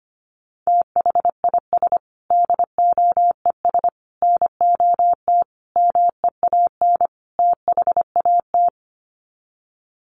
Morse code, heard as a keyed tone, about 25 wpm, 700 Hz. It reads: THIS DOES NOT MEAN THAT
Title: Sets of 5 Words Encore2 25wpm